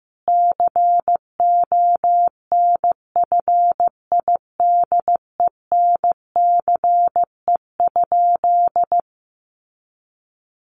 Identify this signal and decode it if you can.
Morse code, heard as a keyed tone, about 15 wpm, 700 Hz. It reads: CONFIDENCE?
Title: Sentences from Top 1500 Words 15wpm